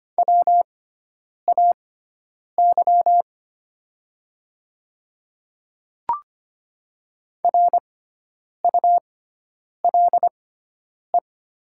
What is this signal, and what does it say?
Morse code, heard as a keyed tone, about 25 wpm, spaced as for 9 wpm, 700 Hz. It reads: WAY RULE
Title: 6x Character Spacing - Top 500 Words - 25wpm